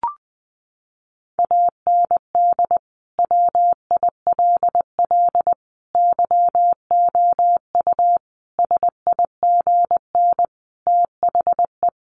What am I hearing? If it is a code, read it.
Morse code, heard as a keyed tone, about 20 wpm, 700 Hz. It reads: AND WILL YOU SIGN THE
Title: Sentences from Top 1400 Words 20wpm